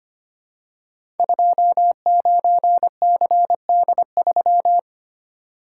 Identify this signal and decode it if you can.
Morse code, heard as a keyed tone, about 25 wpm, 700 Hz. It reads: 29CD3
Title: Code Groups - 5 Letter:Numbers - 25wpm